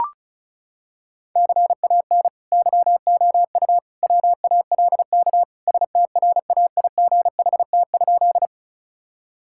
Morse code, heard as a keyed tone, about 36 wpm, 690 Hz. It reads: CAN YOU WALK STRAIGHT?
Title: Sentences from Top 1000 Words - Review Part 6 - 35wpm